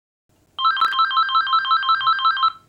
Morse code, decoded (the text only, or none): none